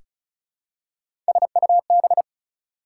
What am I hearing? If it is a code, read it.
Morse code, heard as a keyed tone, about 36 wpm, 700 Hz. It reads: SUB
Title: Prefixes and Suffixes 35wpm